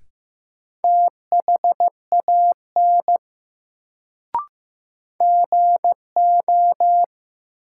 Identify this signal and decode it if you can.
Morse code, heard as a keyed tone, about 15 wpm, 700 Hz. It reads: THAN GO
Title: Sentences from Top 100 Words - New Word Drill - 15wpm